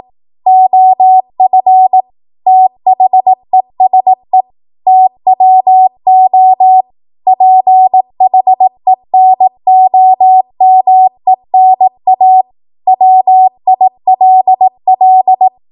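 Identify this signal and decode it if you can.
Morse code, heard as a keyed tone, about 18 wpm, 750 Hz. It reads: OF THESE TWO PHENOMENA WILL
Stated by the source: 18 WPM Morse code practice from a national amateur radio association's code archive.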